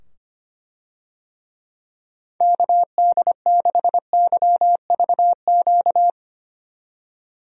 Morse code, heard as a keyed tone, about 25 wpm, 700 Hz. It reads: KD6YVQ
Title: USA Call Signs - 6 Characters - 25wpm